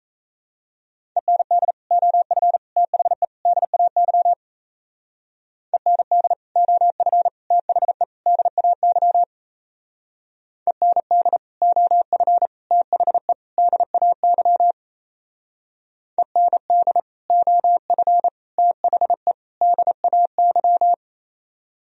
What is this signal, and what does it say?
Morse code, heard as a keyed tone, about 34 wpm, 700 Hz. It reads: END OF THE DAY END OF THE DAY END OF THE DAY END OF THE DAY
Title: Speed-Racing - Sets of 4 Words 28wpm